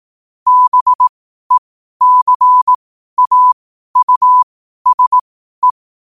Morse code, heard as a keyed tone, about 18 wpm, spaced as for 12 wpm, 1000 Hz. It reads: BECAUSE
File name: qotdfiles / 12wpm